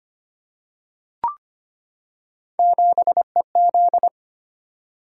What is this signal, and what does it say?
Morse code, heard as a keyed tone, about 25 wpm, 700 Hz. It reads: 7EZ
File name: Three Letter-Number Combinations - 25wpm